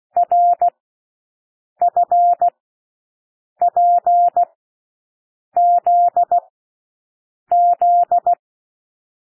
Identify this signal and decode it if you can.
Morse code, heard as a keyed tone, about 16 wpm, spaced as for 6 wpm, 700 Hz. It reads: RFPZZ
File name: cwmp3 / L1 SOUND LETTERS 16 wpm